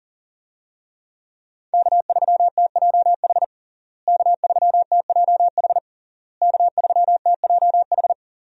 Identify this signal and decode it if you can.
Morse code, heard as a keyed tone, about 37 wpm, 700 Hz. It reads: K3TJH K3TJH K3TJH
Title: USA Call Signs - 5 Characters - Repeated 3x - 40wpm